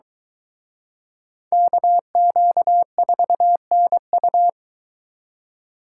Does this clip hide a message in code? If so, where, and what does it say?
Yes: Morse code from 1.5 to 4.5 s, heard as a keyed tone, about 23 wpm, 700 Hz. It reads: KQ4NU